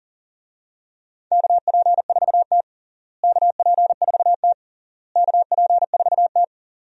Morse code, heard as a keyed tone, about 40 wpm, 700 Hz. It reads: KP4T KP4T KP4T
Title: DXCC-100 Call Signs - 4 Characters - Repeated 3x - 40wpm